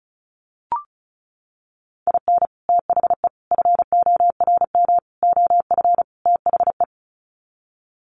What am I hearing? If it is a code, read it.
Morse code, heard as a keyed tone, about 34 wpm, 700 Hz. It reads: IN THE FORM OF THE
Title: Sets of 5 Words Encore2 35wpm